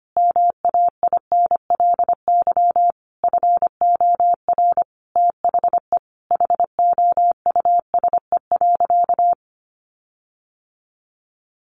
Morse code, heard as a keyed tone, about 25 wpm, 700 Hz. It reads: MAINLY FOR THE HOUSE.
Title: Sentences from Top 1400 Words 25wpm